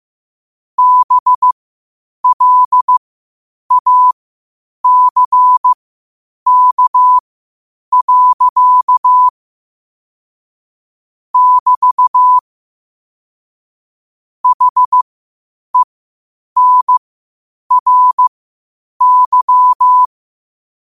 Morse code, heard as a keyed tone, about 15 wpm, spaced as for 8 wpm, 1000 Hz. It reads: BLACK. <BT> HENRY
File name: qotdfiles / 8wpm